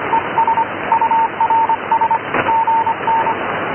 Morse code, heard as a keyed tone, about 25 wpm, 910 Hz, with radio noise. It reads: ES URS GN
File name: noisecanceller fichiers / noise cw